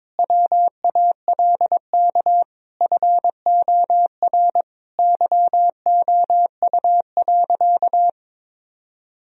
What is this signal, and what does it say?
Morse code, heard as a keyed tone, about 22 wpm, 700 Hz. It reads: WALK FOR YOU.
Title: Sentences from Top 1000 Words - Review Part 1 - 22wpm